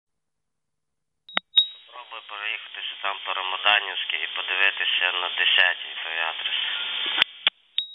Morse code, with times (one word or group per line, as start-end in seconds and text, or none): none